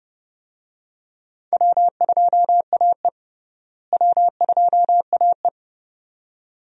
Morse code, heard as a keyed tone, about 30 wpm, 700 Hz. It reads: W2AE W2AE
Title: USA Call Signs - 4 Characters - Repeated 2x - 30wpm